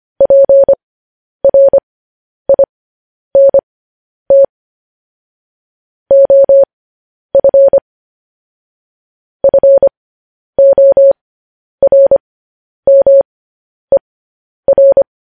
Morse code, heard as a keyed tone, about 25 wpm, spaced as for 10 wpm, 550 Hz. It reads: PRINT OF FORMER
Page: news headlines in morse code at 10 wpm